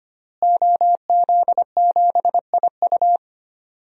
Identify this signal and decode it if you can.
Morse code, heard as a keyed tone, about 25 wpm, 700 Hz. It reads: OZ7IU
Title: DXCC-100 Call Signs - 5 Characters - Repeated 2x - 25wpm